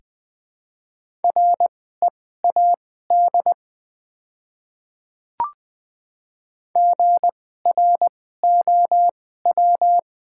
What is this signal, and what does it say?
Morse code, heard as a keyed tone, about 20 wpm, spaced as for 15 wpm, 690 Hz. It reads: READ GROW
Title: 2x Character Spacing - Top 300 Words - 20wpm